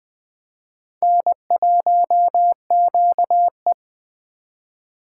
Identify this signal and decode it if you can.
Morse code, heard as a keyed tone, about 20 wpm, 700 Hz. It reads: N1QE
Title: Speed Racing - USA 1x2 Call Signs 17wpm